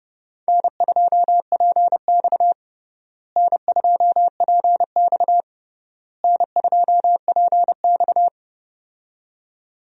Morse code, heard as a keyed tone, about 32 wpm, 700 Hz. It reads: N2PX N2PX N2PX
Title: USA Call Signs - 4 Characters - Repeated 3x - 30wpm